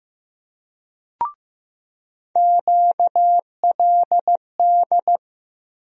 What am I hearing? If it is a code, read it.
Morse code, heard as a keyed tone, about 15 wpm, 700 Hz. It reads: QLD
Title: USA Call Sign Postfixes - 3 Letters 15wpm